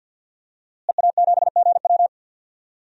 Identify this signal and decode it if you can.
Morse code, heard as a keyed tone, about 47 wpm, 700 Hz. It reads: EA7GW
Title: DXCC-100 Call Signs - 5 Characters - Repeated 2x - 50wpm